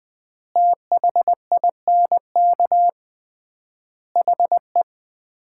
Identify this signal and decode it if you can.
Morse code, heard as a keyed tone, about 21 wpm, 690 Hz. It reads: THINK HE
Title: 3x Word-Spacing - Sets of 5 Words - 20wpm